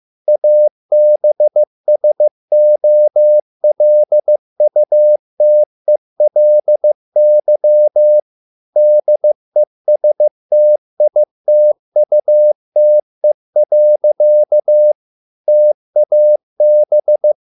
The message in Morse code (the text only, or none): ABSOLUTELY DESTITUTE. TAB